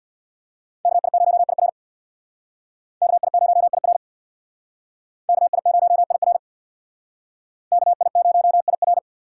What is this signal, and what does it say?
Morse code, heard as a keyed tone, about 63 wpm, 700 Hz. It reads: KI0IR KI0IR KI0IR KI0IR